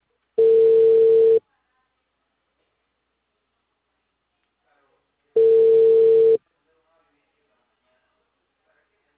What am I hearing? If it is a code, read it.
Ringback tone (China, 450 Hz), with speech.